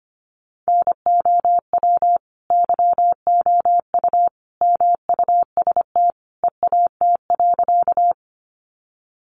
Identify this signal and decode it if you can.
Morse code, heard as a keyed tone, about 25 wpm, 700 Hz. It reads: NOW YOU MUST EAT.
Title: Sentences from Top 300 Words 25wpm